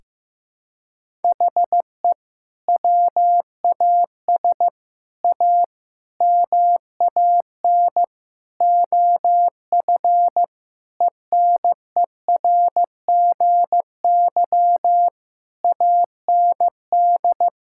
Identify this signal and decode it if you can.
Morse code, heard as a keyed tone, about 15 wpm, 700 Hz. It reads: HE WAS A MAN OF ENERGY AND
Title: Sentences from Top 1000 Words - Review Part 6 - 15wpm